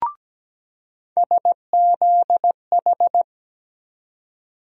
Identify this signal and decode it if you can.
Morse code, heard as a keyed tone, about 17 wpm, 700 Hz. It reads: SZH
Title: USA Call Sign Postfixes - 3 Letters 17wpm